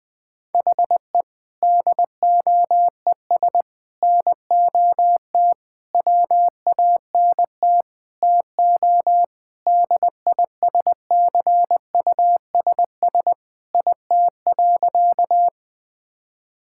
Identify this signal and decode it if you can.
Morse code, heard as a keyed tone, about 20 wpm, 700 Hz. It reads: HE DOES NOT WANT TO DISCUSS IT.
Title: Sentences with 5-Word Phrases - Spoken Beginning - 20wpm